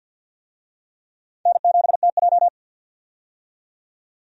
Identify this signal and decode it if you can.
Morse code, heard as a keyed tone, about 48 wpm, 700 Hz. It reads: N7TJ
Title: USA Call Signs - 4 Characters - 50wpm